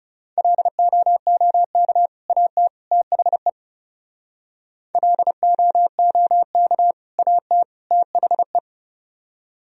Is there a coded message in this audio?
Yes: Morse code from 0.4 to 8.6 s, heard as a keyed tone, about 40 wpm, 700 Hz. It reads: LOOK AT THE LOOK AT THE